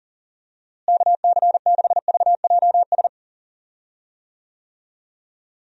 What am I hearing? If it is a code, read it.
Morse code, heard as a keyed tone, about 40 wpm, 700 Hz. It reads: KC6VJS
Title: USA Call Signs - 6 Characters - Repeated 2x - 40wpm